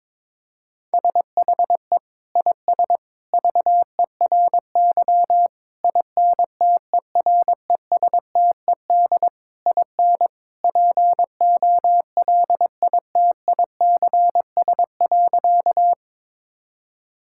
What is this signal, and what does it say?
Morse code, heard as a keyed tone, about 22 wpm, 700 Hz. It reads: SHE IS VERY INTERESTED IN POLITICS.
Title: Sentences from Top 1000 Words - Review Part 6 - 22wpm